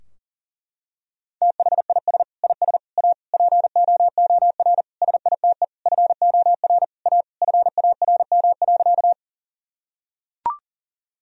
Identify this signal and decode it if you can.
Morse code, heard as a keyed tone, about 37 wpm, 700 Hz. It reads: THIS IS A POOR SITE FOR A FARM.
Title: Sentences from Top 1000 Words - Review Part 4 - 40wpm